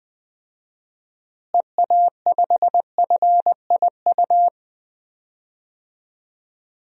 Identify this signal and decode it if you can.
Morse code, heard as a keyed tone, about 20 wpm, 700 Hz. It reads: EA5FIU
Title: DXCC-100 Call Signs - 6 Characters - Repeated 2x - 20wpm